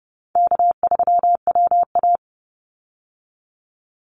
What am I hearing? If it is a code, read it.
Morse code, heard as a keyed tone, about 30 wpm, 700 Hz. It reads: K3WA
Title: Easy Call Signs Repeated 3x - 30wpm